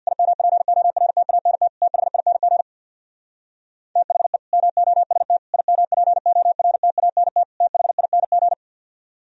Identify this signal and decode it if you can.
Morse code, heard as a keyed tone, about 64 wpm, 700 Hz. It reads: IMPORTANT THING THE MOST IMPORTANT THING